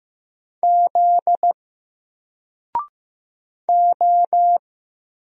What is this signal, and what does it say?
Morse code, heard as a keyed tone, about 15 wpm, 700 Hz. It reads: Z O
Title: Single Letter-Number Alt 15wpm